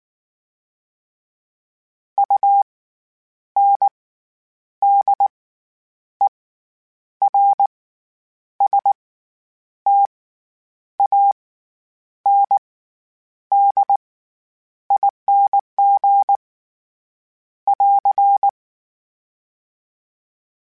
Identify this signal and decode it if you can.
Morse code, heard as a keyed tone, about 19 wpm, spaced as for 8 wpm, 800 Hz. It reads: UNDERSTANDING <AR>
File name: Special INGs / Special INGs-mp3s / SlowToFaster ing @18